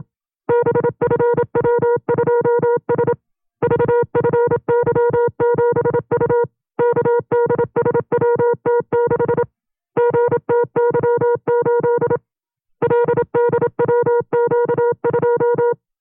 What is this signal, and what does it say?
Morse code, heard as a keyed tone, about 27 wpm, 480 Hz. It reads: BFW2S VFY7U KDSWT6 GTY8 LDWQ2
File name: FreeFiles / CW-practice / 25WPM-side1